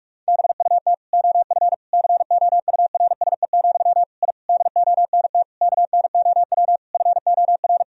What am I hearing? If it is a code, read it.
Morse code, heard as a keyed tone, about 41 wpm, 700 Hz. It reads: BUT OF COURSE, I DONT KNOW FOR
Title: Sentences from Top 1000 Words - Review Part 2 - 45wpm